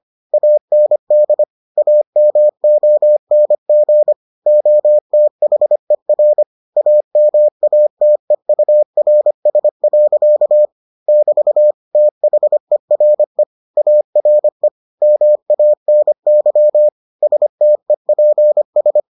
Morse code, heard as a keyed tone, about 25 wpm, 600 Hz. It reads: AND AMONG OTHER AMATEURS. <BT> THERE ARE MANY STEPS